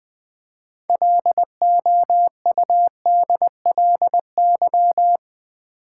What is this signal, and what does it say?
Morse code, heard as a keyed tone, about 20 wpm, 700 Hz. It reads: LOUDLY
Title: Build A Word - 3 Letters At A Time - 6-Letter Words - 20wpm